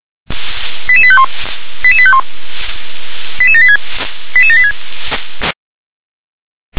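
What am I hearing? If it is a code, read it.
Five-tone selective call (ZVEI), heard as quick runs of five tones, tones of about 70 ms. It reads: 88851 88851 88786 88786
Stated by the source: Received by SDR, ZVEI five-tone selective call 88080.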